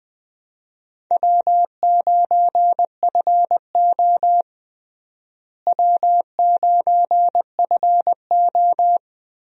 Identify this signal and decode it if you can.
Morse code, heard as a keyed tone, about 21 wpm, 700 Hz. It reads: W9FO W9FO
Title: USA Call Signs - 4 Characters - Repeated 3x - 20wpm